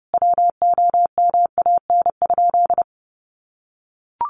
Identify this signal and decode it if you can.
Morse code, heard as a keyed tone, about 30 wpm, 700 Hz. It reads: WOMAN?
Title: Sentences from Top 1100 Words 30wpm